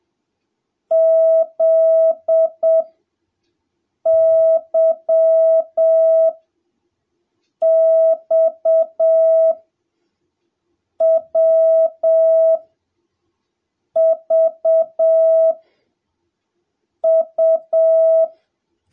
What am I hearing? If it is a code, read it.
Morse code, heard as a keyed tone, about 7 wpm, spaced as for 4 wpm, 650 Hz. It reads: ZYXWVU